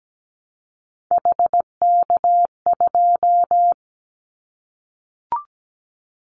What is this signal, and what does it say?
Morse code, heard as a keyed tone, about 17 wpm, 700 Hz. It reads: HK2
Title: Callsign Prefixes for Top 100 DXCC - 17wpm